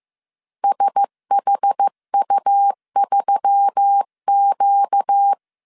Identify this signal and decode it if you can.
Morse code, heard as a keyed tone, about 15 wpm, 800 Hz. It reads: SHU3Q
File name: CW / DET / 1 9 1500C S B 14wpm 3302